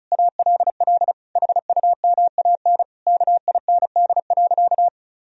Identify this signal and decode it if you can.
Morse code, heard as a keyed tone, about 34 wpm, 700 Hz. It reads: ALL HUMAN KIND.
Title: Sentences from Top 1000 Words - Review Part 4 - 35wpm